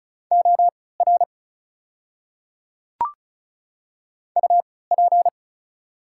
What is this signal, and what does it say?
Morse code, heard as a keyed tone, about 36 wpm, spaced as for 20 wpm, 700 Hz. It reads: OR UP
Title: 3x Character Spacing - Top 100 Words - 35wpm